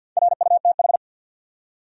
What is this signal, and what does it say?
Morse code, heard as a keyed tone, about 52 wpm, 700 Hz. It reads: RUTH
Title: USA Common Names - 50wpm